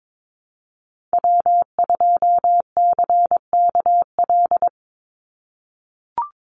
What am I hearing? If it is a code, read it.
Morse code, heard as a keyed tone, about 22 wpm, 700 Hz. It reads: W2CKL
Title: NA SOTA Activator Callsigns 22wpm